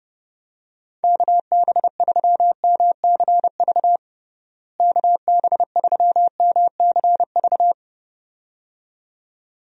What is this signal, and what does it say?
Morse code, heard as a keyed tone, about 32 wpm, 700 Hz. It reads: KB3MCV KB3MCV
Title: USA Call Signs - 6 Characters - Repeated 2x - 30wpm